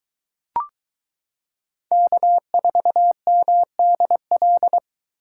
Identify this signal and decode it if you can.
Morse code, heard as a keyed tone, about 22 wpm, 700 Hz. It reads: K4MDL